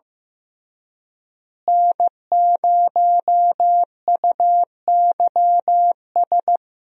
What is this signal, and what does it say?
Morse code, heard as a keyed tone, about 15 wpm, 700 Hz. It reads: N0UYS
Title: USA Call Signs - 5 Characters - Repeated 3x - 15wpm